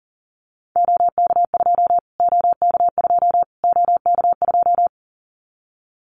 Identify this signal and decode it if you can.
Morse code, heard as a keyed tone, about 40 wpm, 700 Hz. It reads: OK2 OK2 OK2
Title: Callsign Prefixes for Top 25 DXCC - Repeated 3x - 40wpm